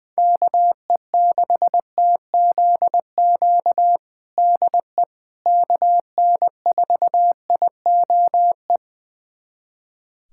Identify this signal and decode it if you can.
Morse code, heard as a keyed tone, about 20 wpm, 700 Hz. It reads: KE6TZQ DE KN4IOE
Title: USA Callsign DE Callsign 20wpm